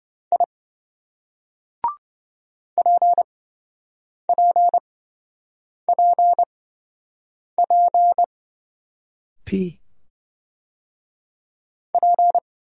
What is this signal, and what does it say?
Morse code, heard as a keyed tone, about 25 wpm, spaced as for 7 wpm, 700 Hz. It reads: I PPPP P